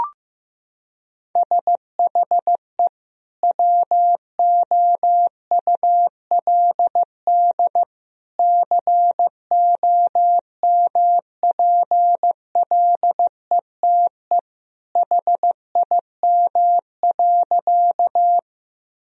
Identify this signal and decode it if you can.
Morse code, heard as a keyed tone, about 16 wpm, 700 Hz. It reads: SHE WOULD COMPLETE HIM.
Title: Sentences from Top 1000 Words - Review Part 2 - 15wpm